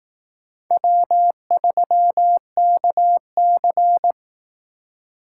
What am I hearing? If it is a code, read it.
Morse code, heard as a keyed tone, about 18 wpm, 700 Hz. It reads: W3KC
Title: Speed Racing - USA Call Signs 15wpm